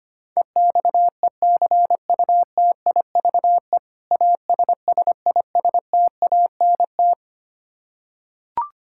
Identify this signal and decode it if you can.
Morse code, heard as a keyed tone, about 25 wpm, 700 Hz. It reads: EXECUTIVE ASSISTANT
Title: Practice QSO - Exchange 3 - More Info - 25wpm